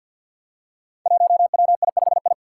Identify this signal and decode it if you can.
Morse code, heard as a keyed tone, about 47 wpm, 700 Hz. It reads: 1WI5I
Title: Five Challenging Letter-Number Combinations 50wpm